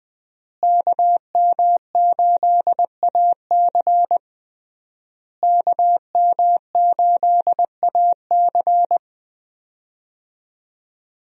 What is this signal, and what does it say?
Morse code, heard as a keyed tone, about 21 wpm, 700 Hz. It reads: KM8AC KM8AC
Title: USA Call Signs - 5 Characters - Repeated 3x - 20wpm